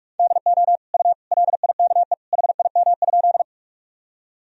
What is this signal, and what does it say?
Morse code, heard as a keyed tone, about 46 wpm, 700 Hz. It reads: DO U LIKE HIM?
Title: Sentences from Top 100 Words and Abbreviations 45wpm